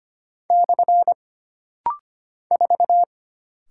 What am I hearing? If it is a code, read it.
Morse code, heard as a keyed tone, about 25 wpm, 700 Hz. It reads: / 4
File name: Single Character (Letter:Number:Punctuation:Prosign) - Rapid-Fire - 25wpm